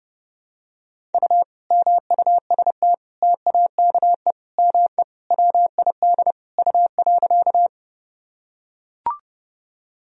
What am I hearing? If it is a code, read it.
Morse code, heard as a keyed tone, about 32 wpm, 700 Hz. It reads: U MUST TAKE ME WID U.
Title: Sentences from Top 200 Words and Abbreviations 30wpm